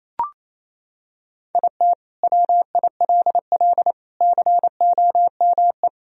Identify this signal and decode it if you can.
Morse code, heard as a keyed tone, about 28 wpm, 700 Hz. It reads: IT WILL COME
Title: Sentences from Top 1000 Words - Review Part 1 - 28wpm